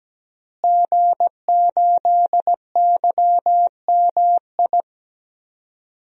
Morse code, heard as a keyed tone, about 17 wpm, 700 Hz. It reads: G8YMI
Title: DXCC-100 Call Signs - 5 Characters - Repeated 3x - 17wpm